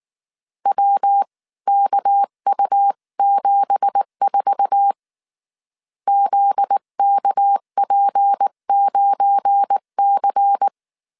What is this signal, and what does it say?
Morse code, heard as a keyed tone, about 19 wpm, 800 Hz. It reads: WKU74 ZKP9C
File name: CW / DET / 1 9 1500C S B 18wpm 2551